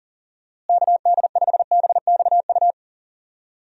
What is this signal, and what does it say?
Morse code, heard as a keyed tone, about 40 wpm, 700 Hz. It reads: KD5BXU
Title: USA Call Signs - 6 Characters - Repeated 3x - 40wpm